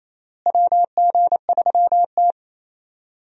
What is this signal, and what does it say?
Morse code, heard as a keyed tone, about 28 wpm, 700 Hz. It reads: WG3T
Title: USA Call Signs - 4 Characters - Repeated 3x - 28wpm